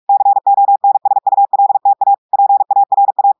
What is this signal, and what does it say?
Morse code, heard as a keyed tone, about 46 wpm, 800 Hz. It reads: CONSULTA PARA
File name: portugais texte en clair 1 vitesse 45 mots minutes